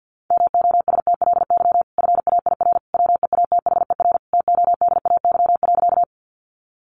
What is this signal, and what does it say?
Morse code, heard as a keyed tone, about 50 wpm, 700 Hz. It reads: MOSTLY FAIR WEATHER TODAY.
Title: Sentences from Top 1300 Words 50wpm